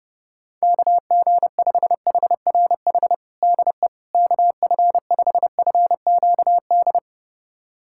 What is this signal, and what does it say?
Morse code, heard as a keyed tone, about 30 wpm, 700 Hz. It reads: KG5HRH DE KF5FQD
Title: USA Callsign DE Callsign - 6 Characters - 30wpm